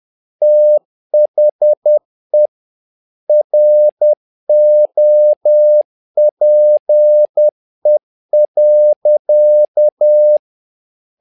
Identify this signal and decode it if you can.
Morse code, heard as a keyed tone, about 10 wpm, 600 Hz. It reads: THE ROPE.
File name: learning center / tower trouble / tower trouble 10wpm